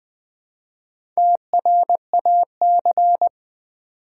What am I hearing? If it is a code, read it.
Morse code, heard as a keyed tone, about 20 wpm, 700 Hz. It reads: TRAC
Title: Build A Word - 1 Letter At A Time - 5-Letter Words - 20wpm